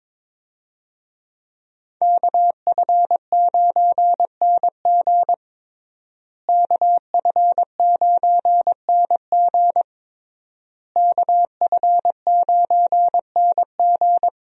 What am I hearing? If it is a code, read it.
Morse code, heard as a keyed tone, about 23 wpm, 700 Hz. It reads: KF9NG KF9NG KF9NG
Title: USA Call Signs - 5 Characters - Repeated 3x - 22wpm